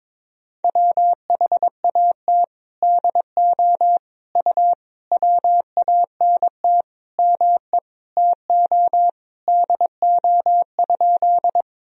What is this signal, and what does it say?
Morse code, heard as a keyed tone, about 22 wpm, 700 Hz. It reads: WHAT DO U WANT ME TO DO?
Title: Sentences from Top 200 Words and Abbreviations 22wpm